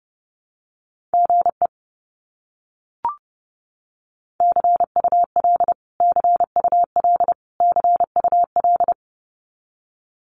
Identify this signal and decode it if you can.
Morse code, heard as a keyed tone, about 30 wpm, 700 Hz. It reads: GE CUL CUL CUL
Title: Abbreviations Repeated 3x - 30wpm